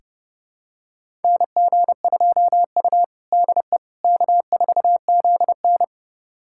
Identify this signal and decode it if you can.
Morse code, heard as a keyed tone, about 30 wpm, 700 Hz. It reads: NG2U DE K4ZN
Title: USA Callsign DE Callsign - 4 Characters - 30wpm